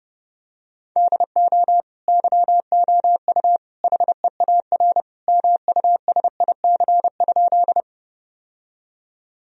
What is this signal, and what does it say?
Morse code, heard as a keyed tone, about 31 wpm, 700 Hz. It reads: DO YOU HEAR MUSIC?
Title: Sentences from Top 1000 Words - Review Part 1 - 30wpm